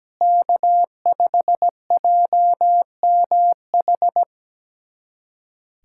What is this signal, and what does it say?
Morse code, heard as a keyed tone, about 17 wpm, 700 Hz. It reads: K5JMH
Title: USA Callsign DE Callsign - 5 Characters - 17wpm